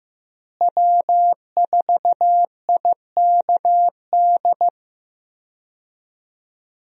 Morse code, heard as a keyed tone, about 15 wpm, 700 Hz. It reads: W4IKD
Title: USA Call Signs - 5 Characters - Repeated 2x - 15wpm